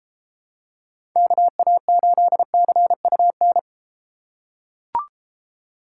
Morse code, heard as a keyed tone, about 30 wpm, 700 Hz. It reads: KA8CUN